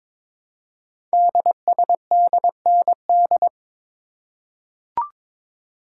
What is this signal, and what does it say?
Morse code, heard as a keyed tone, about 22 wpm, 700 Hz. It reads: DSDND
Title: Code Groups - 5 Letters - 22wpm